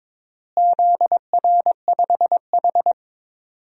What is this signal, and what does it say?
Morse code, heard as a keyed tone, about 22 wpm, 700 Hz. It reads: ZR5H
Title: DXCC-100 Call Signs - 4 Characters - Repeated 2x - 22wpm